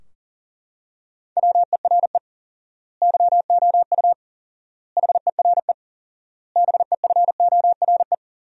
Morse code, heard as a keyed tone, about 40 wpm, 700 Hz. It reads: WERE YOU HERE BEFORE
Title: 4x Word-Spacing - Sentences from Top 200 Words - 40wpm